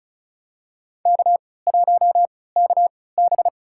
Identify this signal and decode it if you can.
Morse code, heard as a keyed tone, about 34 wpm, spaced as for 20 wpm, 700 Hz. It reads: K1KB
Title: USA Call Signs - 3x Character Spacing - 35wpm